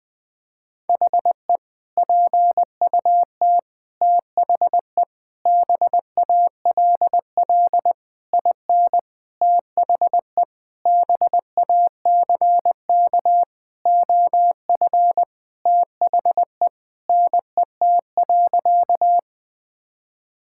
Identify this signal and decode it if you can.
Morse code, heard as a keyed tone, about 20 wpm, 700 Hz. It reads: HE PUT THE BALL IN THE BACK OF THE NET.
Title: Sentences with 8-Word Phrases - 20wpm